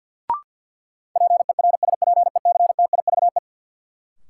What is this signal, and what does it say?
Morse code, heard as a keyed tone, about 53 wpm, 700 Hz. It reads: PERSPECTIVE
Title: Sentences from Top 1200 Words - New Word Drill - 50wpm